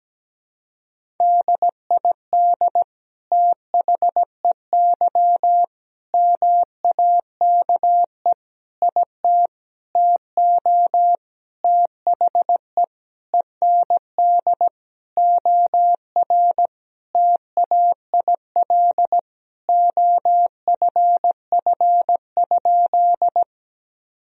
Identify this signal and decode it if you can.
Morse code, heard as a keyed tone, about 17 wpm, 700 Hz. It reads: DID THEY MAKE IT TO THE END OR TAIL OFF?
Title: Sentences from Top 1000 Words - Review Part 3 - 17wpm